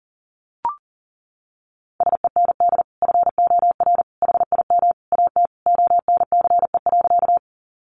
Morse code, heard as a keyed tone, about 40 wpm, 700 Hz. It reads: SEND FOR HIM AT ONCE.
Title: Sentences from Top 1500 Words 40wpm